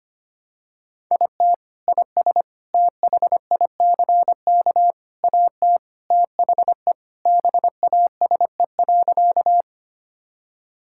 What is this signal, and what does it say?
Morse code, heard as a keyed tone, about 25 wpm, 700 Hz. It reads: IT IS THICK AT THE BASE.
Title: Sentences from Top 1600 Words 25wpm